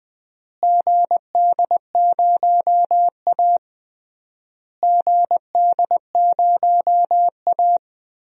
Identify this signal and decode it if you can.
Morse code, heard as a keyed tone, about 20 wpm, 700 Hz. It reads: GD0A GD0A
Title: DXCC-100 Call Signs - 4 Characters - Repeated 3x - 20wpm